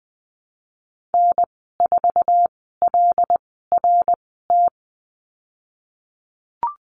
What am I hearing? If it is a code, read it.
Morse code, heard as a keyed tone, about 20 wpm, spaced as for 15 wpm, 700 Hz. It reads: N4LRT
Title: 154 Lesson 25g - Letter p - Call Signs 20wpm